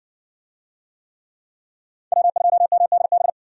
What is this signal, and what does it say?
Morse code, heard as a keyed tone, about 60 wpm, 700 Hz. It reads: W2MDB